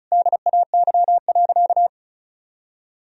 Morse code, heard as a keyed tone, about 36 wpm, 700 Hz. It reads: DAY.
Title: Sentences from Top 1000 Words - Review Part 4 - 35wpm